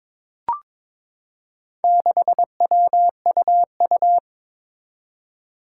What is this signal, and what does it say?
Morse code, heard as a keyed tone, about 22 wpm, 700 Hz. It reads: 6WUU
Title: Four Challenging Letter-Number Combinations 22wpm